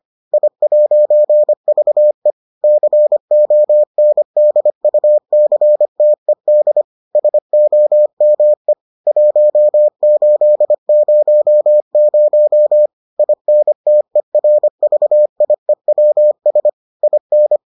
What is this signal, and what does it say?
Morse code, heard as a keyed tone, about 25 wpm, 600 Hz. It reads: I'VE CONDUCTED SOME 1800 INTERVIEWS IN